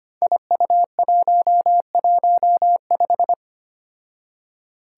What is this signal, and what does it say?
Morse code, heard as a keyed tone, about 25 wpm, 700 Hz. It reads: IU115
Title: Five Challenging Letter-Number Combinations 25wpm